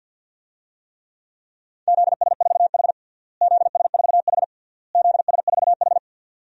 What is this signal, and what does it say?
Morse code, heard as a keyed tone, about 52 wpm, 700 Hz. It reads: ZS4H ZS4H ZS4H
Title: DXCC-100 Call Signs - 4 Characters - Repeated 3x - 50wpm